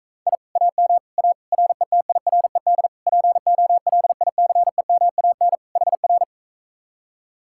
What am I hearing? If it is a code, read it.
Morse code, heard as a keyed tone, about 45 wpm, 700 Hz. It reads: I AM A RETIRED POLICEMAN HR